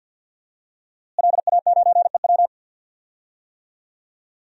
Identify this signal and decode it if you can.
Morse code, heard as a keyed tone, about 51 wpm, 700 Hz. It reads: LA9EW
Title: DXCC-100 Call Signs - 5 Characters - Repeated 2x - 50wpm